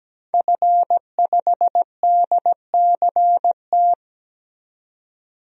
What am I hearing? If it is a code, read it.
Morse code, heard as a keyed tone, about 17 wpm, 700 Hz. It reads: F5DCT
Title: DXCC-100 Call Signs - 5 Characters - 17wpm